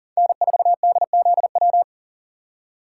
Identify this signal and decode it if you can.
Morse code, heard as a keyed tone, about 37 wpm, 700 Hz. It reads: N4DZW
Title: USA Call Signs - 5 Characters - 2x Character Spacing - 40wpm